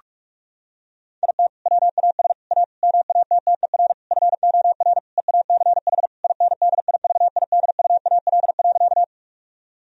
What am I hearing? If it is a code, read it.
Morse code, heard as a keyed tone, about 46 wpm, 700 Hz. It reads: IT WAS A MATTER FOR EACH INDIVIDUAL.
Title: Sentences from Top 1000 Words - Review Part 4 - 45wpm